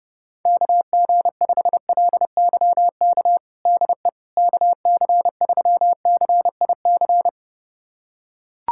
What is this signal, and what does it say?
Morse code, heard as a keyed tone, about 32 wpm, 700 Hz. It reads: KG5LYK DE KC3CIC
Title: USA Callsign DE Callsign - 6 Characters - 30wpm